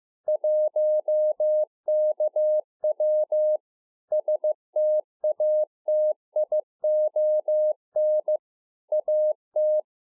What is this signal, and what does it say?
Morse code, heard as a keyed tone, about 15 wpm, 610 Hz. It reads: 1KW STATION AT